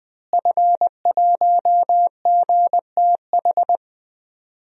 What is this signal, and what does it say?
Morse code, heard as a keyed tone, about 20 wpm, 700 Hz. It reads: F1GTH
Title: DXCC-100 Call Signs - 5 Characters - Repeated 3x - 20wpm